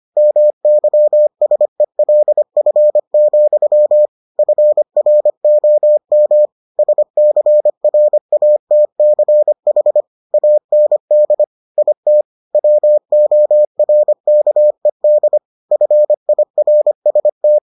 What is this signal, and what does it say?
Morse code, heard as a keyed tone, about 25 wpm, 600 Hz. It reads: MYSELF, FROM SCRATCH AND IT WORKED FIRST